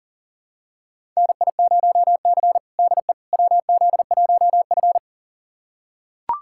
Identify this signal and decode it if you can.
Morse code, heard as a keyed tone, about 37 wpm, 700 Hz. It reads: NI0C DE WZ1F
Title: USA Callsign DE Callsign - 4 Characters - 40wpm